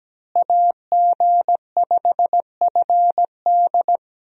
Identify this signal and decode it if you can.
Morse code, heard as a keyed tone, about 17 wpm, 700 Hz. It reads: AG5FD
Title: USA Call Signs - 5 Characters - Repeated 2x - 17wpm